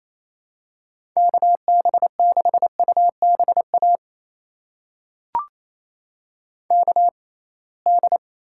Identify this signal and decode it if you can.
Morse code, heard as a keyed tone, about 28 wpm, spaced as for 21 wpm, 700 Hz. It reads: KB6UBA KD
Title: USA Call Signs - 6 Characters - 6x Character Spacing - 28wpm